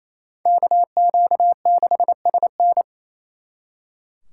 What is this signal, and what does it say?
Morse code, heard as a keyed tone, about 34 wpm, 700 Hz. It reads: KQ6SN
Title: Speed Racing - USA 2x2 Call Signs 28wpm